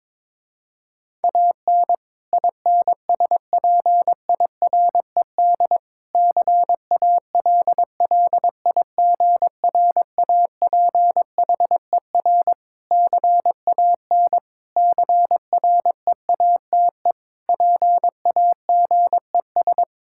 Morse code, heard as a keyed tone, about 22 wpm, 700 Hz. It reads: AN INSPIRED CALLIGRAPHER CAN CREATE PAGES
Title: Pangram Challenge v2 - 22wpm